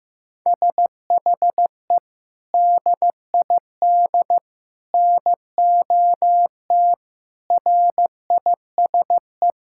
Morse code, heard as a keyed tone, about 15 wpm, 700 Hz. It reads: SHE DID NOT RISE
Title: Sentences from Top 1000 Words - Review Part 6 - 15wpm